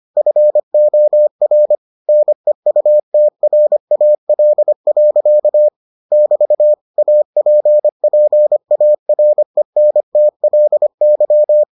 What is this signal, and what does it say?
Morse code, heard as a keyed tone, about 25 wpm, 600 Hz. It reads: FOR NEUTRAL. <BT> APPARENTLY